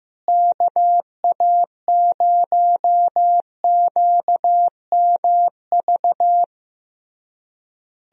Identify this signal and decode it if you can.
Morse code, heard as a keyed tone, about 15 wpm, 700 Hz. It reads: KA0QMV
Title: USA Call Signs - 6 Characters - Repeated 3x - 15wpm